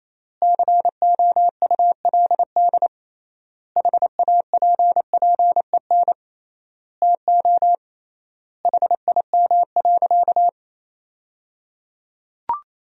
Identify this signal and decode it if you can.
Morse code, heard as a keyed tone, about 28 wpm, 700 Hz. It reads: COULD HAPPEN TO HIM.
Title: 3x Word-Spacing - Sentences from Top 400 Words - 28wpm